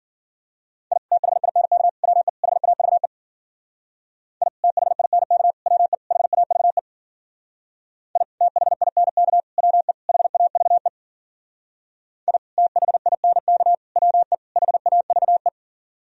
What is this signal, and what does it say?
Morse code, heard as a keyed tone, about 52 wpm, 700 Hz. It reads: I THINK WE HAVE I THINK WE HAVE I THINK WE HAVE I THINK WE HAVE